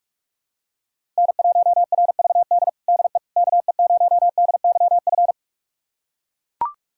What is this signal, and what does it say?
Morse code, heard as a keyed tone, about 42 wpm, 700 Hz. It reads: N1RVD DE KE0DYF
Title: USA Callsign DE Callsign 45wpm